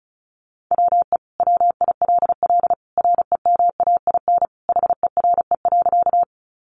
Morse code, heard as a keyed tone, about 34 wpm, 700 Hz. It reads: WE WILL REMAIN HERE.
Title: Sentences from Top 1200 Words 35wpm